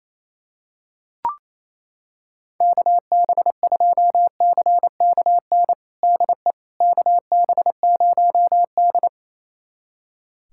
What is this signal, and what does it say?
Morse code, heard as a keyed tone, about 28 wpm, 700 Hz. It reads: KB2CKN DE KB0D
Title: USA Callsign DE Callsign 28wpm